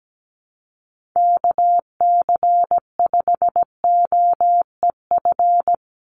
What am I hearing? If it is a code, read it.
Morse code, heard as a keyed tone, about 17 wpm, 700 Hz. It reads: KC5OEF
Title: USA Call Signs 17wpm